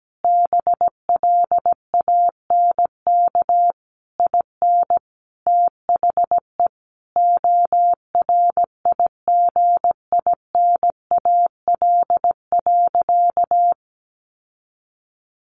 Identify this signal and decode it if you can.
Morse code, heard as a keyed tone, about 17 wpm, 700 Hz. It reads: BLANK IN THE ORIGINAL.
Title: Sentences from Top 1500 Words 17wpm